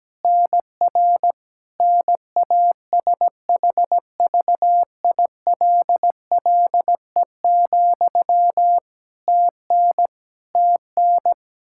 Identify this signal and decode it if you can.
Morse code, heard as a keyed tone, about 17 wpm, 700 Hz. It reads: NR NASHVILLE, TN TN
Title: Practice QSO - Exchange 1 - Beginning - 17wpm